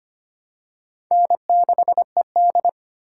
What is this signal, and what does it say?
Morse code, heard as a keyed tone, about 25 wpm, 700 Hz. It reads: N6ED
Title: USA Call Signs - 4 Characters - 3x Character Spacing - 25wpm